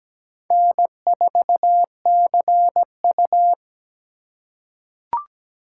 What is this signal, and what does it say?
Morse code, heard as a keyed tone, about 17 wpm, 700 Hz. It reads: N4CU
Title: USA Call Signs - 8x Character Spacing - 17wpm